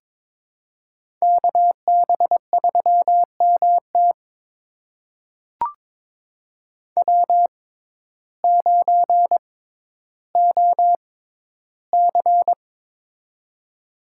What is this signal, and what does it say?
Morse code, heard as a keyed tone, about 22 wpm, spaced as for 12 wpm, 700 Hz. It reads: KB3MT W9OC
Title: USA Call Signs - 6x Character Spacing - 22wpm